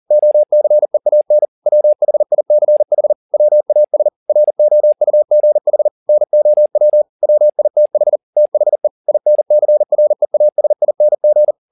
Morse code, heard as a keyed tone, about 40 wpm, 600 Hz. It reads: OCEAN WHICH WAS ROUGH NOW WITH THE INCREASING
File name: the old man and the sea 2 40wpm